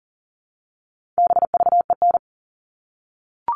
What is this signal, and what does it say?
Morse code, heard as a keyed tone, about 40 wpm, 700 Hz. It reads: BVEN
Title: All Cut Numbers - 4-Digit 40wpm